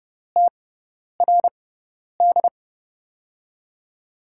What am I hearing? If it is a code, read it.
Morse code, heard as a keyed tone, about 30 wpm, spaced as for 10 wpm, 700 Hz. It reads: TRD
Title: USA Call Signs - 5 Characters - 6x Character Spacing - 30wpm